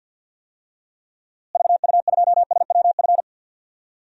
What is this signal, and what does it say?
Morse code, heard as a keyed tone, about 48 wpm, 700 Hz. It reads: VU2SWF
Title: DXCC-100 Call Signs - 6 Characters - Repeated 2x - 50wpm